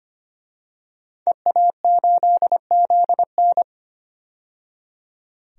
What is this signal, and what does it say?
Morse code, heard as a keyed tone, about 25 wpm, 700 Hz. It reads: EA8ZN
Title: DXCC-100 Call Signs - 5 Characters - 25wpm